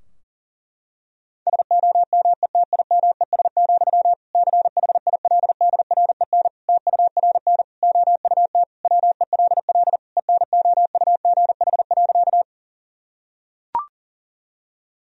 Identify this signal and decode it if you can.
Morse code, heard as a keyed tone, about 40 wpm, 700 Hz. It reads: SOMETIMES, CHILDREN TURN OUT WELL ENOUGH.
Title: Sentences from Top 1000 Words - Review Part 3 - 40wpm